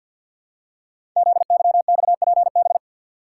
Morse code, heard as a keyed tone, about 52 wpm, 700 Hz. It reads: ZYXPB
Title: Code Groups - 5 Letter:Numbers - 50wpm